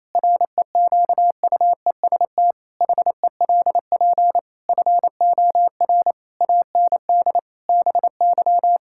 Morse code, heard as a keyed tone, about 28 wpm, 700 Hz. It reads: REQUEST HELP FOR AND BY
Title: Pangram Challenge v2 - 28wpm